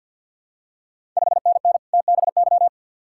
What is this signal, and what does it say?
Morse code, heard as a keyed tone, about 53 wpm, 700 Hz. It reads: 5NN TBY
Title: Ontario QSO Party Exchange 50wpm